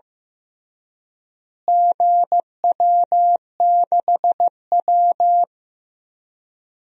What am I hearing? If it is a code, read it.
Morse code, heard as a keyed tone, about 15 wpm, 700 Hz. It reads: GW6W
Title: DXCC-100 Call Signs - 4 Characters - Repeated 2x - 15wpm